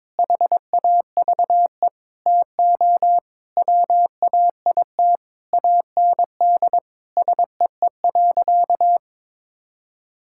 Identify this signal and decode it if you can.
Morse code, heard as a keyed tone, about 22 wpm, 700 Hz. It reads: HAVE TO WAIT AND SEE.
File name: Sentences with 7-Word Phrases - 22wpm